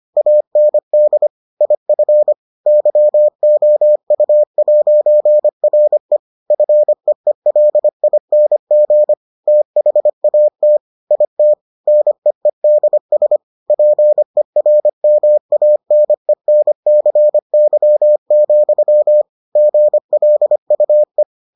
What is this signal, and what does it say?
Morse code, heard as a keyed tone, about 25 wpm, 600 Hz. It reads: AND IF YOU'RE FEELING THAT IT NEEDS PERMANENCY, GLUE